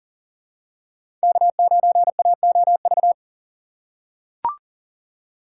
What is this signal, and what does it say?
Morse code, heard as a keyed tone, about 40 wpm, 700 Hz. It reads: K9AOV
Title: USA Call Signs - 5 Characters - 3x Character Spacing - 40wpm